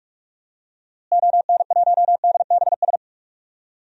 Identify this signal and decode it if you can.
Morse code, heard as a keyed tone, about 46 wpm, 700 Hz. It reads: ON1DBS
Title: DXCC-100 Call Signs - 6 Characters - Repeated 2x - 45wpm